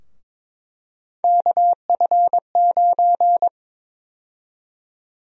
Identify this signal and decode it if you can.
Morse code, heard as a keyed tone, about 22 wpm, 700 Hz. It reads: KF9
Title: USA Call Sign Prefixes - 22wpm